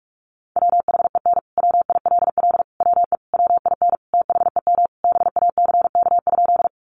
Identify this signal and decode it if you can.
Morse code, heard as a keyed tone, about 44 wpm, 700 Hz. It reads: WHEN WILL WE WIN THEM BACK?
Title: Sentences from Top 1500 Words 45wpm